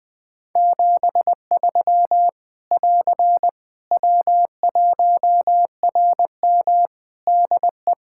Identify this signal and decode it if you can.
Morse code, heard as a keyed tone, about 20 wpm, 700 Hz. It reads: 73 <AR> W1RM DE
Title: Practice QSO - Exchange 4 - Ending - 20wpm